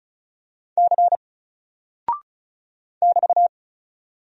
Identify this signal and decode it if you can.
Morse code, heard as a keyed tone, about 36 wpm, 700 Hz. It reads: C <BT>
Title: Single Character (Letter:Number:Punctuation:Prosign) Alt 35wpm